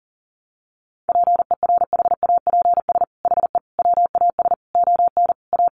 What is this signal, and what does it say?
Morse code, heard as a keyed tone, about 40 wpm, 700 Hz. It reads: PERHAPS HE WAS ON A
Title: Sentences from Top 1100 Words 40wpm